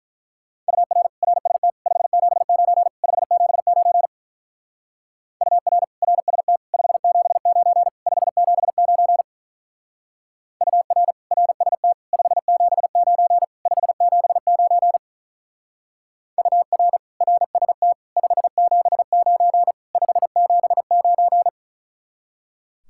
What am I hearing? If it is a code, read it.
Morse code, heard as a keyed tone, about 43 wpm, 700 Hz. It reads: UR RST 579 579 UR RST 579 579 UR RST 579 579 UR RST 579 579
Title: Speed-Racing Common QSO Phrases - 35wpm